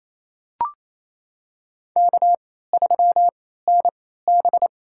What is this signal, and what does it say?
Morse code, heard as a keyed tone, about 27 wpm, spaced as for 16 wpm, 700 Hz. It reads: K3NB
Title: USA Call Signs - 4 Characters - 3x Character Spacing - 28wpm